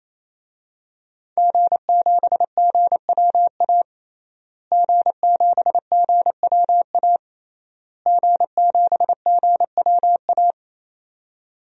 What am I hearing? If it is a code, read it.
Morse code, heard as a keyed tone, about 27 wpm, 700 Hz. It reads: G7GWA G7GWA G7GWA
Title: DXCC-100 Call Signs - 5 Characters - Repeated 3x - 28wpm